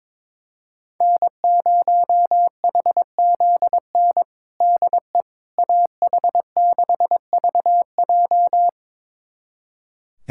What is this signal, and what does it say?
Morse code, heard as a keyed tone, about 22 wpm, 700 Hz. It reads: N0HZN DE AH6VJ
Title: USA Callsign DE Callsign - 5 Characters - 22wpm